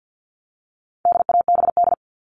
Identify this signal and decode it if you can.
Morse code, heard as a keyed tone, about 50 wpm, 700 Hz. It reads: DABD
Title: All Cut Numbers - 4-Digit 50wpm